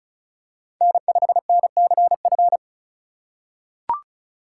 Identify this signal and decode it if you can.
Morse code, heard as a keyed tone, about 36 wpm, 700 Hz. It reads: N5NCF
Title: USA Call Signs - 5 Characters - 2x Character Spacing - 35wpm